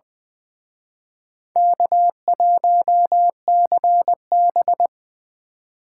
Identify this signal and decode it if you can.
Morse code, heard as a keyed tone, about 24 wpm, 700 Hz. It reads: K1CB
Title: Speed Racing - USA 1x2 Call Signs 20wpm